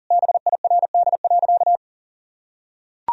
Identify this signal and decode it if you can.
Morse code, heard as a keyed tone, about 37 wpm, 700 Hz. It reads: BIRD.
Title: Sentences from Top 1000 Words - Review Part 2 - 40wpm